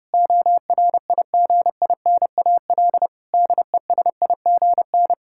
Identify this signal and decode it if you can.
Morse code, heard as a keyed tone, about 30 wpm, 700 Hz. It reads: ORIGINAL DESIGN
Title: Sentences from Top 1000 Words - Review Part 6 - 30wpm